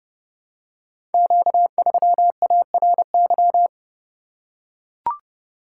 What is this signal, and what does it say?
Morse code, heard as a keyed tone, about 30 wpm, 700 Hz. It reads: Q3ARY
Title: Code Groups - 5 Letter:Numbers - 30wpm